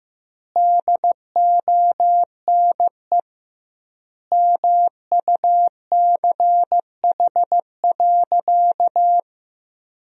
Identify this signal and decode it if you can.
Morse code, heard as a keyed tone, about 15 wpm, 700 Hz. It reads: DONE MUCH.
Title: 2x Word-Spacing - Sentences from Top 500 Words - 15wpm